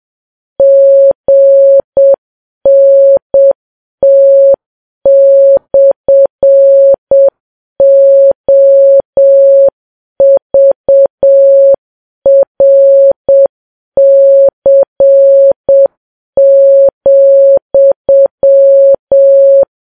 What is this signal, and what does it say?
Morse code, heard as a keyed tone, about 7 wpm, 550 Hz. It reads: GNT/OVRC,
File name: Files Web tls / Morse on9fv / morse7kps